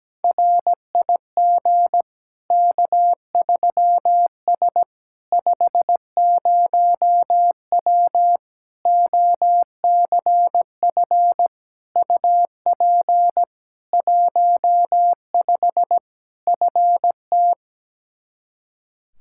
Morse code, heard as a keyed tone, about 17 wpm, 700 Hz. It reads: RIG K3S 50W OCF UP 15 FT
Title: K1USN SST QSO Practice 17wpm